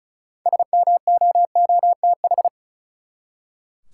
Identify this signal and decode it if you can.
Morse code, heard as a keyed tone, about 34 wpm, 700 Hz. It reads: SMOOTH
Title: Six Letter Words - 35wpm